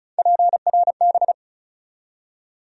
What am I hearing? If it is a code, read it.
Morse code, heard as a keyed tone, about 34 wpm, 700 Hz. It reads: PRB
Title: USA Call Sign Postfixes - 3 Letters 35wpm